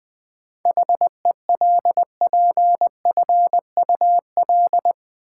Morse code, heard as a keyed tone, about 20 wpm, 700 Hz. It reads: HELPFUL
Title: Seven Letter Words - 20wpm